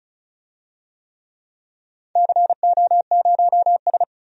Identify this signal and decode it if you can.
Morse code, heard as a keyed tone, about 36 wpm, 700 Hz. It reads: CO0S
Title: DXCC-100 Call Signs - 4 Characters - 35wpm